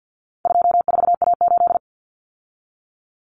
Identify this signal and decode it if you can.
Morse code, heard as a keyed tone, about 50 wpm, 700 Hz. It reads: 24A8
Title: Common Cut Numbers - 4-Digit 50wpm